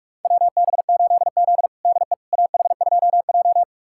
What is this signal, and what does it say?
Morse code, heard as a keyed tone, about 44 wpm, 690 Hz. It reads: WB8Z DE AH2J
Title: USA Callsign DE Callsign - 4 Characters - 45wpm